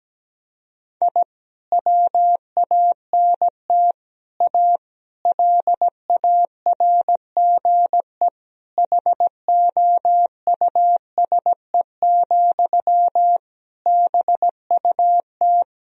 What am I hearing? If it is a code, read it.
Morse code, heard as a keyed tone, about 17 wpm, 700 Hz. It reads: I WANT A LARGE HOUSE, BUT
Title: Sentences from Top 1000 Words - Review Part 1 - 17wpm